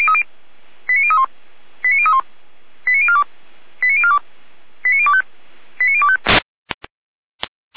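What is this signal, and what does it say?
Five-tone selective call (ZVEI), heard as quick runs of five tones, tones of about 70 ms. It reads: 89031 89031 89042 89042 89025 89025
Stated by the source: Five-tone ZVEI selective call received by SDR, fire brigade Orlamünde (feuerwehr-orlamuende).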